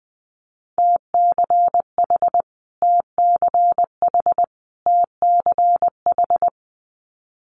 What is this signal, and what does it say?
Morse code, heard as a keyed tone, about 20 wpm, 700 Hz. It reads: TCH TCH TCH
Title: Multi-letter Phonemes Repeated 3x 20wpm